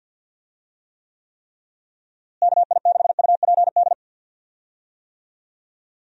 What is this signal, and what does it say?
Morse code, heard as a keyed tone, about 52 wpm, 700 Hz. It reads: KI6UPD
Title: USA Call Signs - 6 Characters - Repeated 3x - 50wpm